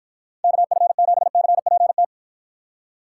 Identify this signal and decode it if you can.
Morse code, heard as a keyed tone, about 53 wpm, 700 Hz. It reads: KF7CPT